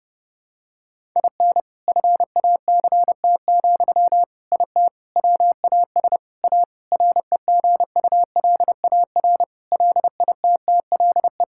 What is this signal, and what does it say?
Morse code, heard as a keyed tone, about 32 wpm, 700 Hz. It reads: IN FACT, IT WAS A REGULAR LITTLE
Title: Sentences from Top 1000 Words - Review Part 6 - 30wpm